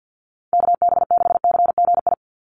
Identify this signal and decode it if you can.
Morse code, heard as a keyed tone, about 50 wpm, 700 Hz. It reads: KB6CGI
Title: USA Call Signs 50wpm